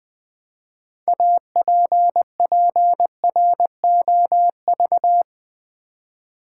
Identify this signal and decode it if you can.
Morse code, heard as a keyed tone, about 20 wpm, 700 Hz. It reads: APPROV
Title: Build A Word - 3 Letters At A Time - 8-Letter Words - 20wpm